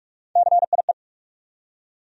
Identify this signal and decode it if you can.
Morse code, heard as a keyed tone, about 46 wpm, 700 Hz. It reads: CIE
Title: USA Call Sign Postfixes - 3 Letters 45wpm